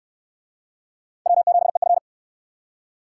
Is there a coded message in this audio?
Yes: Morse code from 1.3 to 2.0 s, heard as a keyed tone, about 66 wpm, 700 Hz. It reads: W7EF